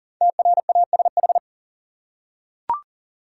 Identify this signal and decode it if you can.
Morse code, heard as a keyed tone, about 40 wpm, 700 Hz. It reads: TRASH
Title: Five Letter Words - 40wpm